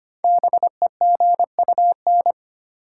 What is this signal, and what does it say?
Morse code, heard as a keyed tone, about 25 wpm, 700 Hz. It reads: BEGUN
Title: Sentences from Top 1000 Words - Review Part 6 - New Word Drill - 25wpm